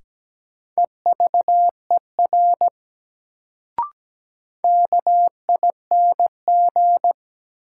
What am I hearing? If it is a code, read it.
Morse code, heard as a keyed tone, about 17 wpm, 700 Hz. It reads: EVER KING
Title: Sentences from Top 400 Words - New Word Drill - 17wpm